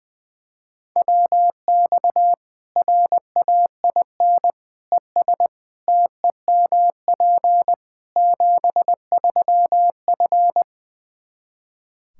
Morse code, heard as a keyed tone, about 21 wpm, 700 Hz. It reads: WX RAIN ES TEMP 73F
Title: Practice QSO - Exchange 2 - Info - 20wpm